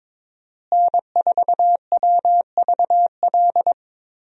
Morse code, heard as a keyed tone, about 22 wpm, 700 Hz. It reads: N4WVL
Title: USA Call Signs - 5 Characters - Repeated 2x - 22wpm